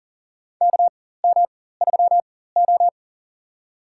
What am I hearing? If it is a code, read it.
Morse code, heard as a keyed tone, about 40 wpm, spaced as for 19 wpm, 700 Hz. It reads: KM3O
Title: USA Call Signs - 4x Character Spacing - 40wpm